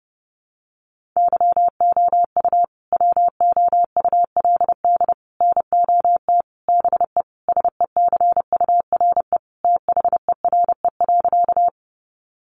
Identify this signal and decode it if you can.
Morse code, heard as a keyed tone, about 30 wpm, 700 Hz. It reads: YOU WOULD NOT BE SECURE THERE.
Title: Sentences from Top 1200 Words 30wpm